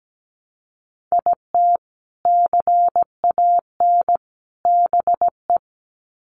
Morse code, heard as a keyed tone, about 17 wpm, 700 Hz. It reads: IT CAN BE
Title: Sets of 3 Words 17wpm